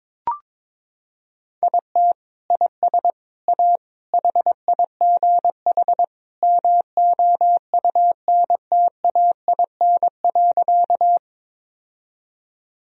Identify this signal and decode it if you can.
Morse code, heard as a keyed tone, about 22 wpm, 700 Hz. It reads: IT IS A HIGH MOUNTAIN.
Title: Sentences from Top 1000 Words - Review Part 1 - 22wpm